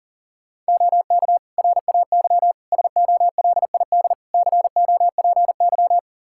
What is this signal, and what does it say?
Morse code, heard as a keyed tone, about 37 wpm, 700 Hz. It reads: OK RAY SOLID COPY
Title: Practice QSO - Exchange 3 - More Info - 40wpm